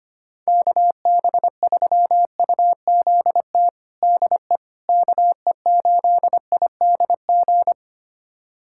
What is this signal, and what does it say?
Morse code, heard as a keyed tone, about 25 wpm, 700 Hz. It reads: KB3UZT DE KE8IDG
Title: USA Callsign DE Callsign 25wpm